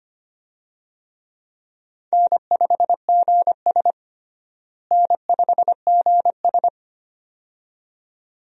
Morse code, heard as a keyed tone, about 25 wpm, 700 Hz. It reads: N5GS N5GS
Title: USA Call Signs - 4 Characters - Repeated 2x - 25wpm